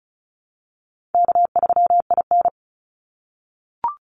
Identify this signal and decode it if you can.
Morse code, heard as a keyed tone, about 34 wpm, 700 Hz. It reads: K3IN
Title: Call Signs - 35wpm